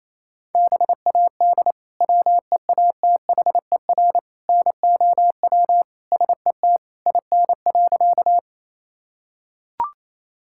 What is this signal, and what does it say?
Morse code, heard as a keyed tone, about 28 wpm, 700 Hz. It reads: BAD WEATHER NOW SET IN.
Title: Sentences from Top 1000 Words - Review Part 5 - 28wpm